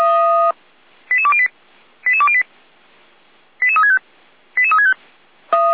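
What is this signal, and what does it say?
Five-tone selective call (ZVEI), heard as quick runs of five tones, tones of about 70 ms. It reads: 88208 88208 88365 88365
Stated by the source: ZVEI five-tone call received on SDR.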